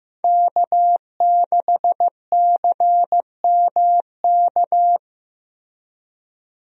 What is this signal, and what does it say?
Morse code, heard as a keyed tone, about 15 wpm, 700 Hz. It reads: K6CMK
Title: USA Call Signs - 5 Characters - Repeated 3x - 15wpm